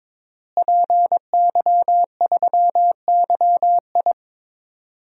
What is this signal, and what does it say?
Morse code, heard as a keyed tone, about 22 wpm, 700 Hz. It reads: PY3YI
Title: DXCC-100 Call Signs - 5 Characters - Repeated 3x - 22wpm